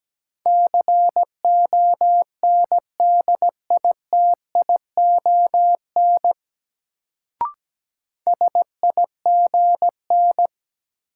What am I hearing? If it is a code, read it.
Morse code, heard as a keyed tone, about 17 wpm, 700 Hz. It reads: CONDITION SIGN
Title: Sentences from Top 1000 Words - New Word Drill - 17wpm